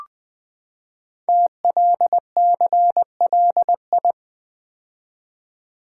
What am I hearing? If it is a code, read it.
Morse code, heard as a keyed tone, about 20 wpm, 700 Hz. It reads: TLCLI
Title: Code Groups - 5 Letter:Numbers - 20wpm